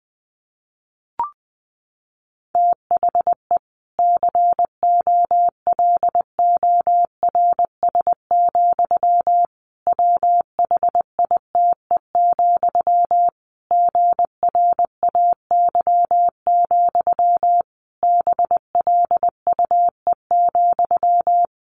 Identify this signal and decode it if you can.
Morse code, heard as a keyed tone, about 20 wpm, 700 Hz. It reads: THE COLORS, WHITE, GRAY, BLUE,
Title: A Sound Of Thunder 20wpm